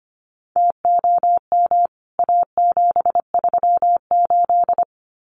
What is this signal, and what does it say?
Morse code, heard as a keyed tone, about 25 wpm, 700 Hz. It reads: TOM A738
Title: CWT Exchange 25wpm